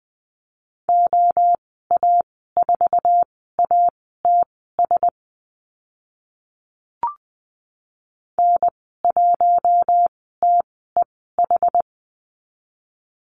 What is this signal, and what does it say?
Morse code, heard as a keyed tone, about 20 wpm, spaced as for 15 wpm, 700 Hz. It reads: OA4ATS N1TEH
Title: 069 Lesson 12g - Letter d - Call Signs 20wpm